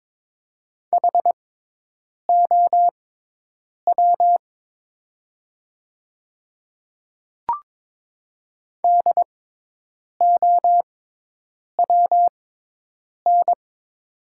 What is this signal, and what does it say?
Morse code, heard as a keyed tone, about 22 wpm, spaced as for 8 wpm, 700 Hz. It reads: HOW DOWN
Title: 6x Character Spacing - Top 200 Words - 22wpm